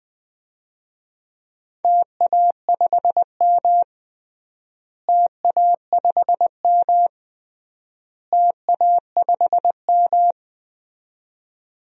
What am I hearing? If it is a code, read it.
Morse code, heard as a keyed tone, about 21 wpm, 700 Hz. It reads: TA5M TA5M TA5M
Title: DXCC-100 Call Signs - 4 Characters - Repeated 3x - 20wpm